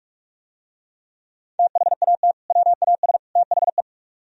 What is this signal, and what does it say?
Morse code, heard as a keyed tone, about 44 wpm, 700 Hz. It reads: THAT WAS THE